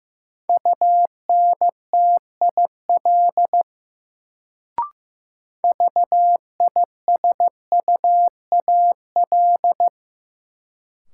Morse code, heard as a keyed tone, about 15 wpm, 700 Hz. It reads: UNTIL VISUAL
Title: Sets of 5 Words Encore - New Word Drill - 15wpm